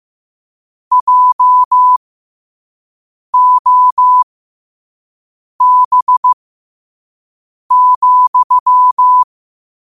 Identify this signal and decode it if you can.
Morse code, heard as a keyed tone, about 15 wpm, spaced as for 5 wpm, 1000 Hz. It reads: JOB,
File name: qotdfiles / 5wpm